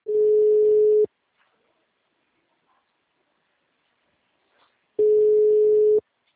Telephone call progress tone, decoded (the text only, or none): ringback tone